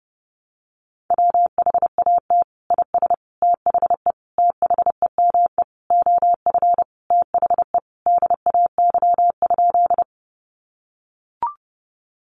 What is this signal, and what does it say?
Morse code, heard as a keyed tone, about 30 wpm, 700 Hz. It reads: WHAT IS THE THEME OF THE DAY?
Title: Sentences from Top 1100 Words 30wpm